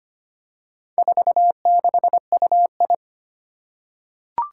Morse code, heard as a keyed tone, about 25 wpm, 700 Hz. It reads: 46UI
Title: Four Challenging Letter-Number Combinations 25wpm